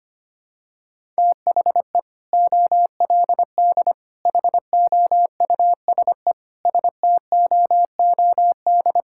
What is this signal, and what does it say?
Morse code, heard as a keyed tone, about 25 wpm, 700 Hz. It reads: THE OLD HOUSE STOOD
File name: practice / 25wpm / Sentences with 2-Word Phrases - 25wpm